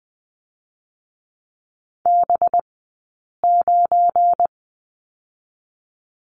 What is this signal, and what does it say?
Morse code, heard as a keyed tone, about 20 wpm, 700 Hz. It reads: B 9
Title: 183 Lesson 30d - Letter b - Two Characters 20wpm